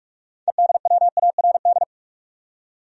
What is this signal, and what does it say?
Morse code, heard as a keyed tone, about 46 wpm, 700 Hz. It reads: EDWARD
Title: USA Common Names - 45wpm